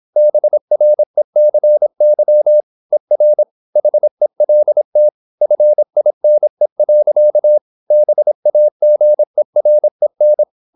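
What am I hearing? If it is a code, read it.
Morse code, heard as a keyed tone, about 26 wpm, 600 Hz. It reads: BRECY ER HELT FINE. BAGEREN
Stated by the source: High-speed (QRQ) Morse at 26 wpm.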